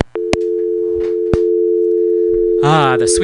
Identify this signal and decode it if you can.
Dial tone (North America), 350 + 440 Hz, with speech.